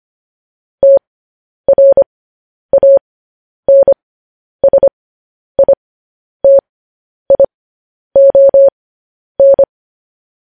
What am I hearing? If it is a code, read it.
Morse code, heard as a keyed tone, about 25 wpm, spaced as for 10 wpm, 550 Hz. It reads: TRANSITION
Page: news headlines in morse code at 10 wpm